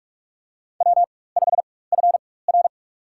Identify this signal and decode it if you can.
Morse code, heard as a keyed tone, about 47 wpm, spaced as for 21 wpm, 700 Hz. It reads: W5FR
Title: USA Call Signs - 4 Characters - 4x Character Spacing - 45wpm